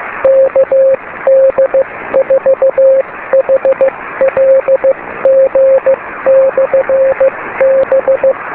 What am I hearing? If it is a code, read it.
Morse code, heard as a keyed tone, about 15 wpm, 540 Hz, with radio noise. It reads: KD4HLG/B
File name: beacons / kd4hlg-b